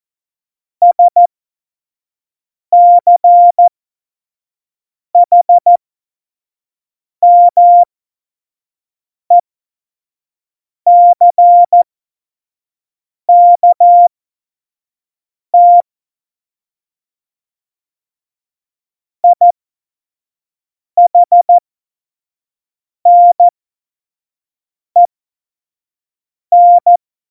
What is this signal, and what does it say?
Morse code, heard as a keyed tone, about 14 wpm, spaced as for 5 wpm, 700 Hz. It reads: SCHMECKT IHNEN
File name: Morsekurs / Witze / Witz 02. Juli 2019, 25 F70